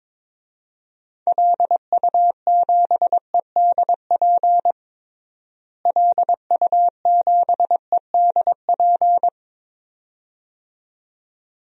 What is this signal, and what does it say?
Morse code, heard as a keyed tone, about 22 wpm, 700 Hz. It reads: LU7EDP LU7EDP
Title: DXCC-100 Call Signs - 6 Characters - Repeated 2x - 22wpm